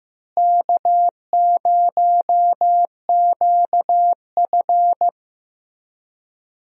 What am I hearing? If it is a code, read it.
Morse code, heard as a keyed tone, about 15 wpm, 700 Hz. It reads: K0QF
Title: USA Call Signs - 4 Characters - Repeated 2x - 15wpm